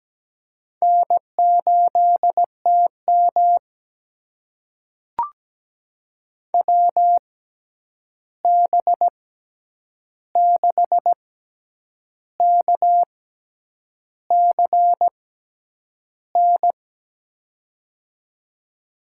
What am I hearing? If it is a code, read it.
Morse code, heard as a keyed tone, about 17 wpm, spaced as for 8 wpm, 700 Hz. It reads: N8TM WB6KCN
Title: USA Call Signs - 6x Character Spacing - 17wpm